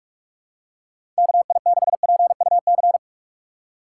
Morse code, heard as a keyed tone, about 46 wpm, 700 Hz. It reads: KI6PUC
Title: USA Call Signs - 3x Character Spacing - 45wpm